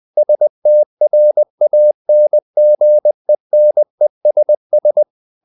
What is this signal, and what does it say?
Morse code, heard as a keyed tone, about 20 wpm, 600 Hz. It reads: STRANGENESS
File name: the old man and the sea part 2 20wpm